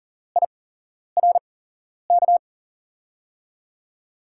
Morse code, heard as a keyed tone, about 40 wpm, spaced as for 11 wpm, 700 Hz. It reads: IRK
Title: USA Call Signs - 6 Characters - 8x Character Spacing - 40wpm